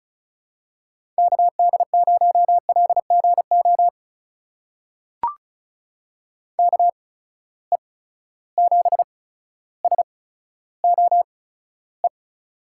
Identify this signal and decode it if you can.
Morse code, heard as a keyed tone, about 36 wpm, spaced as for 15 wpm, 700 Hz. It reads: KD0LGO KE7SOE
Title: USA Call Signs - 6 Characters - 8x Character Spacing - 35wpm